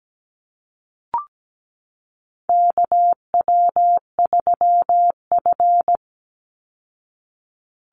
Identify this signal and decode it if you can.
Morse code, heard as a keyed tone, about 17 wpm, 700 Hz. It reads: KW3F
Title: Call Signs Repeated 2x - 17wpm